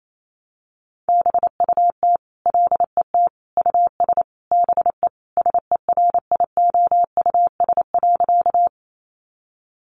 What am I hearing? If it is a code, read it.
Morse code, heard as a keyed tone, about 28 wpm, 700 Hz. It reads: BUT LET US BE SERIOUS.
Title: Sentences from Top 1100 Words 28wpm